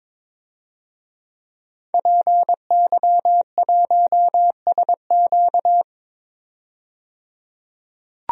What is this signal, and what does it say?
Morse code, heard as a keyed tone, about 22 wpm, 700 Hz. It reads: PY1SQ
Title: DXCC-100 Call Signs - 5 Characters - 22wpm